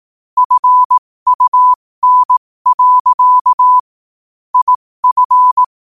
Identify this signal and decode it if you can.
Morse code, heard as a keyed tone, about 18 wpm, spaced as for 15 wpm, 1000 Hz. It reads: FUN. IF
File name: qotdfiles / 15wpm